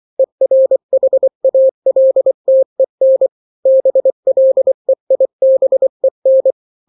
Morse code, heard as a keyed tone, about 24 wpm, 530 Hz. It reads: ERHALTEN BLEIBEN
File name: daily / 23wpm